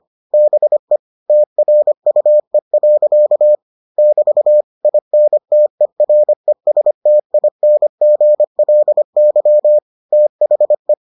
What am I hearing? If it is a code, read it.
Morse code, heard as a keyed tone, about 25 wpm, 600 Hz. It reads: BE TRUE. <BT> INTERESTINGLY THE